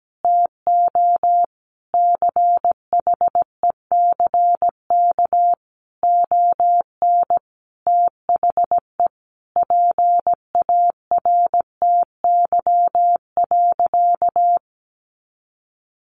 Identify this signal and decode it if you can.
Morse code, heard as a keyed tone, about 17 wpm, 700 Hz. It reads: TO CHECK ON THE PARTY.
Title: Sentences from Top 1400 Words 17wpm